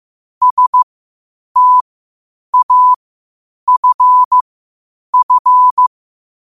Morse code, heard as a keyed tone, about 15 wpm, spaced as for 8 wpm, 1000 Hz. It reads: STAFF
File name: qotdfiles / 8wpm